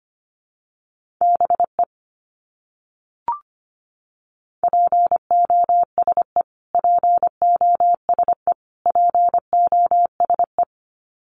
Morse code, heard as a keyed tone, about 25 wpm, 700 Hz. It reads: BE POSE POSE POSE
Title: Top 500 Words Repeated 25wpm